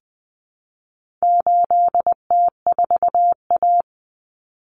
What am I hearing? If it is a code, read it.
Morse code, heard as a keyed tone, about 20 wpm, 700 Hz. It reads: 8T4A
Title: Common Cut Numbers - 4-Digit 20wpm